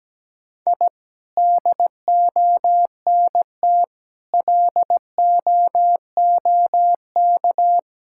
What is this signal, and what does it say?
Morse code, heard as a keyed tone, about 17 wpm, 700 Hz. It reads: I DONT LOOK
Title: Sentences from Top 1000 Words - Review Part 5 - 17wpm